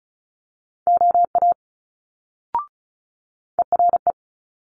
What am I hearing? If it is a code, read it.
Morse code, heard as a keyed tone, about 36 wpm, 700 Hz. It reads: OA ERE
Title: Multi-letter Phonemes Quick Fire 35wpm